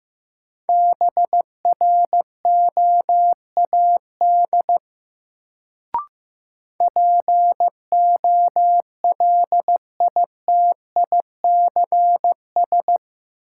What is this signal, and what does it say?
Morse code, heard as a keyed tone, about 15 wpm, 700 Hz. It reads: BROAD POLITICS
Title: Sentences from Top 1000 Words - New Word Drill - 15wpm